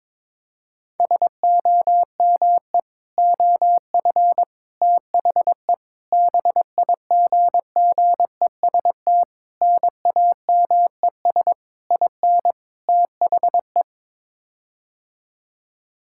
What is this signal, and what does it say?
Morse code, heard as a keyed tone, about 22 wpm, 700 Hz. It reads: SOME OF THE BIGGEST NAMES IN THE
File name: practice / 22wpm / Sets of 7 Words - 22wpm